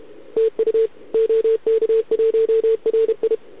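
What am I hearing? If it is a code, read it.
Morse code, heard as a keyed tone, about 30 wpm, 440 Hz. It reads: TU OK1RI
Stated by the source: Contest CW from OK1RI on 80 m during the OKOM DX contest, recorded by a QRP station using a KX3 and a wire antenna.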